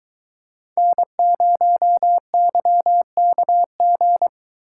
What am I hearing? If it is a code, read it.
Morse code, heard as a keyed tone, about 23 wpm, 700 Hz. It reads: N0YKG